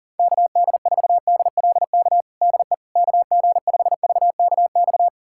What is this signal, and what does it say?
Morse code, heard as a keyed tone, about 40 wpm, 700 Hz. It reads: KD4DLK DE KG5VKX
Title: USA Callsign DE Callsign - 6 Characters - 40wpm